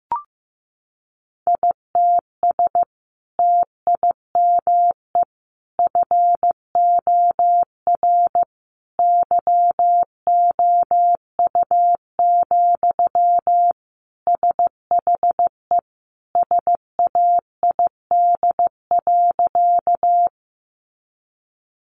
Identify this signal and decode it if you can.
Morse code, heard as a keyed tone, about 15 wpm, 700 Hz. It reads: ITS TIME FOR YOU, SHE SAID.
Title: Sentences from Top 600 Words 15wpm